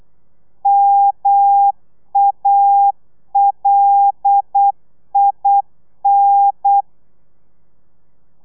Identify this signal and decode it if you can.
Morse code, heard as a keyed tone, about 8 wpm, 800 Hz. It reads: MALIN
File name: telegrafia / i2tfj / Lez5 35